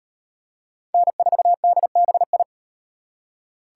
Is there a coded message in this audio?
Yes: Morse code from 0.9 to 2.4 s, heard as a keyed tone, about 34 wpm, 700 Hz. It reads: N4DBI